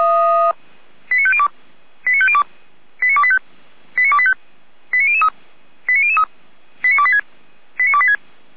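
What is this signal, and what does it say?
Five-tone selective call (ZVEI), heard as quick runs of five tones, tones of about 70 ms. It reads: 89502 89502 89285 89285 89003 89003 89286 89286
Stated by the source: ZVEI five-tone call 89280 received by SDR.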